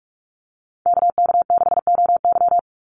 Morse code, heard as a keyed tone, about 44 wpm, 700 Hz. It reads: KK6OY
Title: USA Call Signs 45wpm